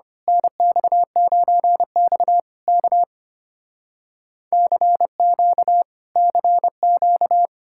Morse code, heard as a keyed tone, about 29 wpm, 700 Hz. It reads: NX9X K CQ CQ
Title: Speed-Racing Common QSO Phrases - 25wpm